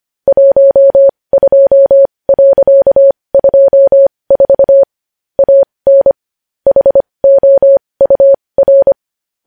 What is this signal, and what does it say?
Morse code, heard as a keyed tone, about 25 wpm, spaced as for 20 wpm, 550 Hz. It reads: 12.24 AN HOUR
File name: podcast / News headlines-1x-25x20